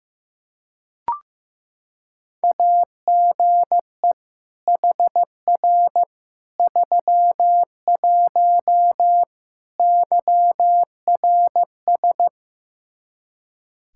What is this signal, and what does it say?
Morse code, heard as a keyed tone, about 16 wpm, 700 Hz. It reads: AGE HR 31 YRS
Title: Practice QSO - Exchange 3 - More Info - 15wpm